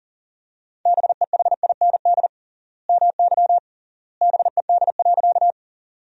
Morse code, heard as a keyed tone, about 40 wpm, 700 Hz. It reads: BEHIND MY BED.
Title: 3x Word-Spacing - Sentences from Top 500 Words - 40wpm